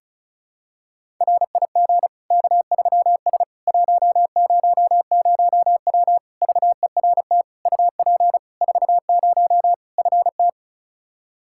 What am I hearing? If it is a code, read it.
Morse code, heard as a keyed tone, about 36 wpm, 700 Hz. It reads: RIG K3S 100W VERT UP 40 FT
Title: K1USN SST QSO Practice 35wpm